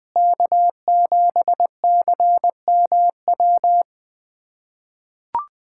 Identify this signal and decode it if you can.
Morse code, heard as a keyed tone, about 20 wpm, 700 Hz. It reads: K7CMW
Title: USA Callsign DE Callsign 20wpm